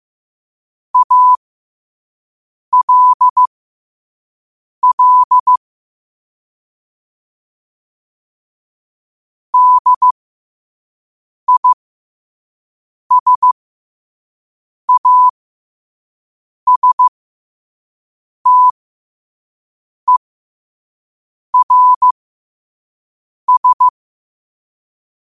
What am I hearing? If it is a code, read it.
Morse code, heard as a keyed tone, about 15 wpm, spaced as for 5 wpm, 1000 Hz. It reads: ALL DISASTERS
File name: qotdfiles / 5wpm